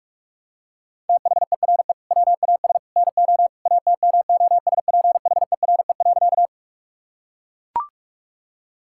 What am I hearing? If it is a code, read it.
Morse code, heard as a keyed tone, about 41 wpm, 700 Hz. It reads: THERE WAS NO ATMOSPHERE.
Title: Sentences from Top 1600 Words 45wpm